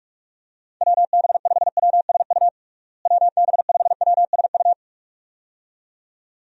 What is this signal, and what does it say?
Morse code, heard as a keyed tone, about 42 wpm, 700 Hz. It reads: WB5WSU WB5WSU
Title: USA Call Signs - 6 Characters - Repeated 2x - 45wpm